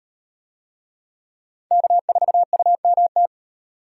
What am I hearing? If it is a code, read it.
Morse code, heard as a keyed tone, about 38 wpm, 700 Hz. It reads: K4UMT